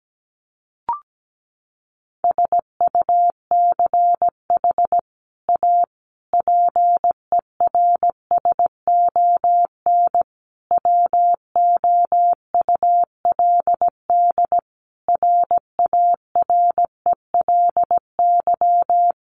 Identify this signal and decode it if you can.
Morse code, heard as a keyed tone, about 17 wpm, 700 Hz. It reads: SUCH A PERSON WOULD RARELY
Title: Sentences from Top 1400 Words 17wpm